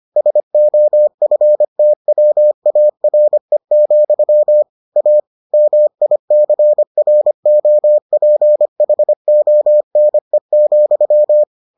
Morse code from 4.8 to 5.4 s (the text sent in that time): A